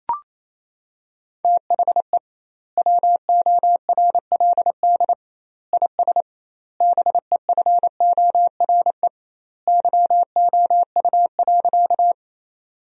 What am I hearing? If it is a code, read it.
Morse code, heard as a keyed tone, about 27 wpm, 700 Hz. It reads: THE WORLD IS BEFORE YOU.
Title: 2x Word-Spacing - Sentences from Top 200 Words - 28wpm